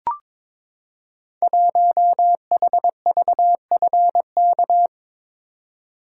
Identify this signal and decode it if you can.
Morse code, heard as a keyed tone, about 22 wpm, 700 Hz. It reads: 1HVFK
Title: Code Groups - 5 Letter:Numbers - 22wpm